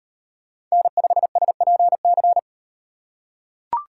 Morse code, heard as a keyed tone, about 35 wpm, 700 Hz. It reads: N5SPC